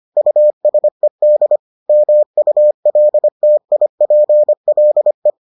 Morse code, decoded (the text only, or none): USED MULTIPLE